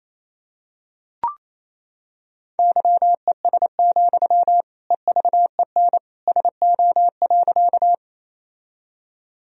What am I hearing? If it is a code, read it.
Morse code, heard as a keyed tone, about 28 wpm, 700 Hz. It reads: YES, EVEN SO.
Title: Sentences from Top 1000 Words - Review Part 1 - 28wpm